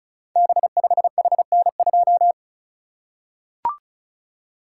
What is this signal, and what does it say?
Morse code, heard as a keyed tone, about 36 wpm, 690 Hz. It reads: B5HN2
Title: Code Groups - 5 Letter:Numbers - 35wpm